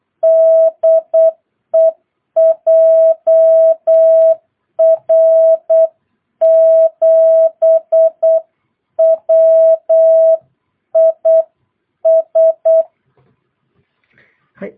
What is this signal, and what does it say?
Morse code, heard as a keyed tone, about 8 wpm, 650 Hz. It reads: DE JR7WIS
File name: jh8cbh / 2022cw